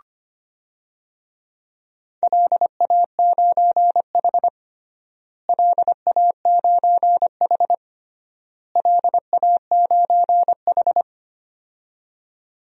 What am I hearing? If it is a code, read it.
Morse code, heard as a keyed tone, about 25 wpm, 700 Hz. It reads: LA9H LA9H LA9H
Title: DXCC-100 Call Signs - 4 Characters - Repeated 3x - 25wpm